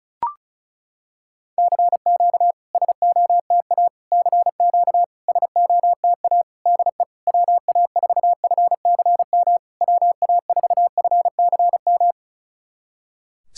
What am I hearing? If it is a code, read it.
Morse code, heard as a keyed tone, about 36 wpm, 700 Hz. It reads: CQ SOTA CQ SOTA DE WA4FCM WA4FCM
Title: SOTA QSO Practice 35wpm